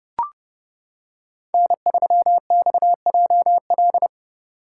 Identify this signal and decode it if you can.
Morse code, heard as a keyed tone, about 29 wpm, 700 Hz. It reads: N3XJL